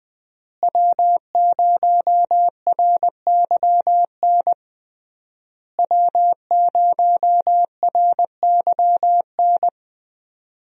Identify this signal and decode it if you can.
Morse code, heard as a keyed tone, about 20 wpm, 700 Hz. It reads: W0RYN W0RYN
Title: USA Call Signs - 5 Characters - Repeated 3x - 20wpm